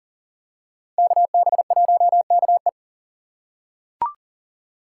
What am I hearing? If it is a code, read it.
Morse code, heard as a keyed tone, about 40 wpm, 700 Hz. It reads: KB1KE
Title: USA Call Signs - 5 Characters - 4x Character Spacing - 40wpm